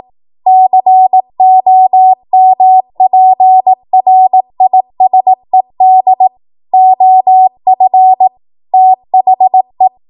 Morse code, heard as a keyed tone, about 18 wpm, 750 Hz. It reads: COMPRISED OF THE
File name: Archive / 18 WPM / 18WPM